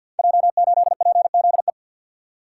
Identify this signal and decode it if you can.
Morse code, heard as a keyed tone, about 52 wpm, 700 Hz. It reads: J8PZE
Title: Code Groups - 5 Letter:Numbers - 50wpm